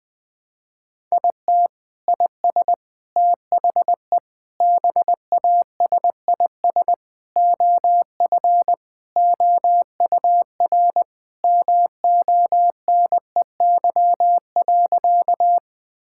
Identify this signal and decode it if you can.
Morse code, heard as a keyed tone, about 20 wpm, 700 Hz. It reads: IT IS THE BASIS OF OUR MONEY.
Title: Sentences from Top 1000 Words - Review Part 5 - 20wpm